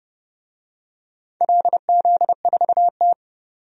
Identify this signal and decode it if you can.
Morse code, heard as a keyed tone, about 30 wpm, 700 Hz. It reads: LZ4T
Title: DXCC-100 Call Signs - 4 Characters - 30wpm